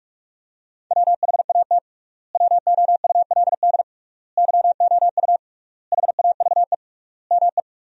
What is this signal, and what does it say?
Morse code, heard as a keyed tone, about 47 wpm, 700 Hz. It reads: WHAT WOULD YOU HAVE ME
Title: 3x Word-Spacing - Sentences from Top 300 Words - 45wpm